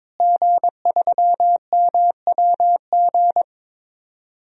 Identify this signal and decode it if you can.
Morse code, heard as a keyed tone, about 22 wpm, 700 Hz. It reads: G3MWG
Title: DXCC-100 Call Signs - 5 Characters - Repeated 3x - 22wpm